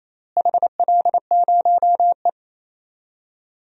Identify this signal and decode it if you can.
Morse code, heard as a keyed tone, about 27 wpm, 700 Hz. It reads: HL0E
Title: DXCC-100 Call Signs - 4 Characters - Repeated 2x - 28wpm